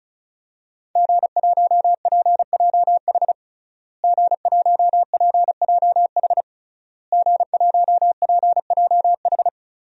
Morse code, heard as a keyed tone, about 34 wpm, 700 Hz. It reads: G1PJH G1PJH G1PJH
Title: DXCC-100 Call Signs - 5 Characters - Repeated 3x - 35wpm